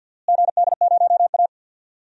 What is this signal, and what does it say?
Morse code, heard as a keyed tone, about 51 wpm, 700 Hz. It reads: GD0A
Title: DXCC-100 Call Signs - 4 Characters - Repeated 2x - 50wpm